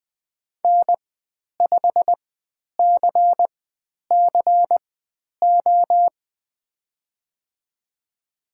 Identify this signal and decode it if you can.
Morse code, heard as a keyed tone, about 20 wpm, spaced as for 10 wpm, 700 Hz. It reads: N5CCO
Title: 109 Lesson 18g - Letter m - Call Signs 20wpm